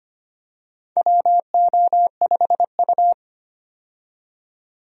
Morse code, heard as a keyed tone, about 25 wpm, 700 Hz. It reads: WO5U
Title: USA Call Signs - 4 Characters - 25wpm